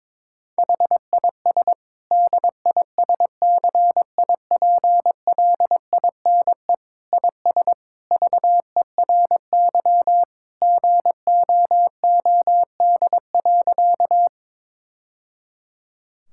Morse code, heard as a keyed tone, about 22 wpm, 700 Hz. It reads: HIS DISCIPLINE IS VERY GOOD.
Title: Sentences from Top 1000 Words - Review Part 6 - 22wpm